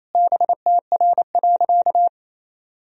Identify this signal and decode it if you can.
Morse code, heard as a keyed tone, about 28 wpm, 700 Hz. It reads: BTR.
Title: Sentences from Top 400 Words and Abbreviations 28wpm